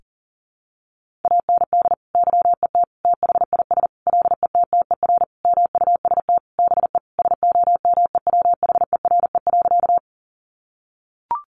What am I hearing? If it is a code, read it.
Morse code, heard as a keyed tone, about 40 wpm, 700 Hz. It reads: AND YET THIS LETTER MUST BE SOMEWHERE.
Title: Sentences from Top 1400 Words 40wpm